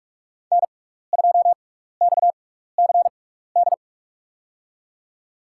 Morse code, heard as a keyed tone, about 46 wpm, spaced as for 16 wpm, 700 Hz. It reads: N2XCD
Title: USA Call Signs - 6x Character Spacing - 45wpm